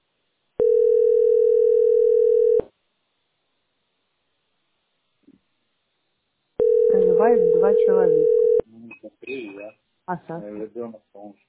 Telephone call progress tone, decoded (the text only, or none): ringback tone